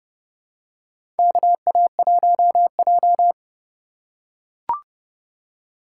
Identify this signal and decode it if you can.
Morse code, heard as a keyed tone, about 31 wpm, 700 Hz. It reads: KA1J
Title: USA Call Signs - 3x Character Spacing - 30wpm